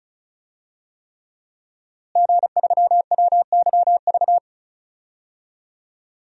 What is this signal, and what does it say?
Morse code, heard as a keyed tone, about 34 wpm, 700 Hz. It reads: G3WYV
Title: DXCC-100 Call Signs - 5 Characters - Repeated 3x - 35wpm